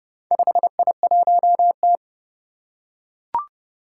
Morse code, heard as a keyed tone, about 30 wpm, 700 Hz. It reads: 5I1T
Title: Four Challenging Letter-Number Combinations 30wpm